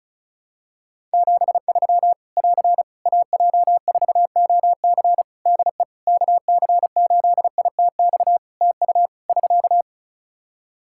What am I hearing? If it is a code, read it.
Morse code, heard as a keyed tone, about 34 wpm, 700 Hz. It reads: 73 <AR> AJ4OC DE KC8ITX TU <SK>
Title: Practice QSO - Exchange 4 - Ending - 35wpm